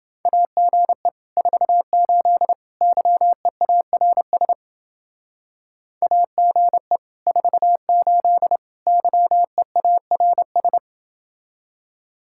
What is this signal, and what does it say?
Morse code, heard as a keyed tone, about 29 wpm, 700 Hz. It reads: AGE 48 YEARS AGE 48 YEARS